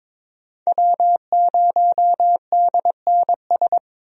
Morse code, heard as a keyed tone, about 22 wpm, 700 Hz. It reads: W0DNS
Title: USA Call Signs - 5 Characters - 22wpm